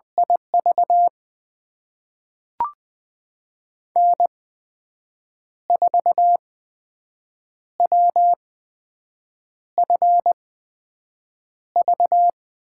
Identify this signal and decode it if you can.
Morse code, heard as a keyed tone, about 20 wpm, spaced as for 6 wpm, 700 Hz. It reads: IV N4WFV
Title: USA Call Signs - 5 Characters - 8x Character Spacing - 20wpm